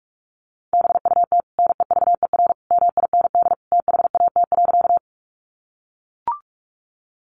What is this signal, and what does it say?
Morse code, heard as a keyed tone, about 44 wpm, 700 Hz. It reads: BUT NEVER MIND THAT.
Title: Sentences from Top 500 Words 45wpm